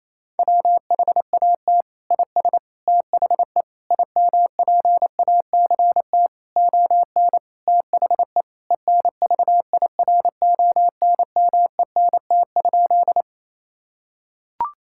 Morse code, heard as a keyed tone, about 28 wpm, 700 Hz. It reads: WHAT IS THE IMPACT ON THE ENVIRONMENT?
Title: Sentences from Top 1000 Words - Review Part 6 - 28wpm